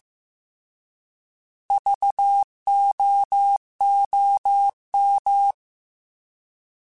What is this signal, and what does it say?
Morse code, heard as a keyed tone, about 15 wpm, 800 Hz. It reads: VOOM
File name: Alphabetical / Alphabetical mp3s / Vs @13